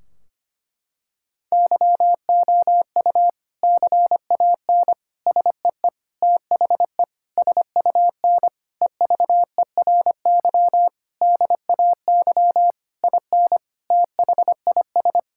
Morse code, heard as a keyed tone, about 25 wpm, 700 Hz. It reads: YOU CAN SEE THE SUN EVERY DAY IN THIS
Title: Sentences from Top 1600 Words 25wpm